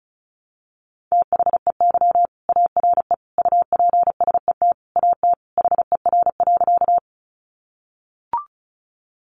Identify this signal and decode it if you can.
Morse code, heard as a keyed tone, about 34 wpm, 700 Hz. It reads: THEY ARE UPSET AT HER.
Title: Sentences from Top 800 Words 35wpm